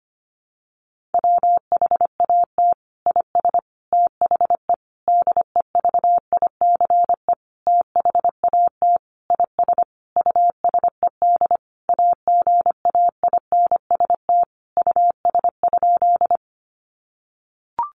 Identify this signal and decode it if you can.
Morse code, heard as a keyed tone, about 25 wpm, 700 Hz. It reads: WHAT IS THE DEVICE THAT IS USED AGAINST US?
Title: Sentences from Top 1500 Words 25wpm